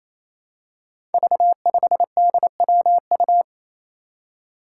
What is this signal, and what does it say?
Morse code, heard as a keyed tone, about 28 wpm, 700 Hz. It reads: V5DWU
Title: Five Challenging Letter-Number Combinations 28wpm